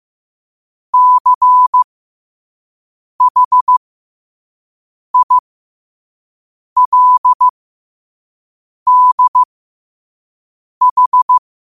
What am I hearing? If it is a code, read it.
Morse code, heard as a keyed tone, about 15 wpm, spaced as for 5 wpm, 1000 Hz. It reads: CHILDH
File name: qotdfiles / 5wpm